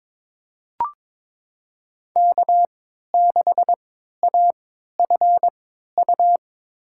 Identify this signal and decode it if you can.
Morse code, heard as a keyed tone, about 21 wpm, spaced as for 12 wpm, 700 Hz. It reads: K6AFU
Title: USA Call Signs - 5 Characters - 3x Character Spacing - 22wpm